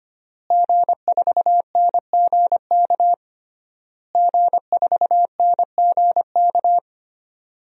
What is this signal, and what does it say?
Morse code, heard as a keyed tone, about 25 wpm, 700 Hz. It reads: G4NGK G4NGK
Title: DXCC-100 Call Signs - 5 Characters - Repeated 3x - 25wpm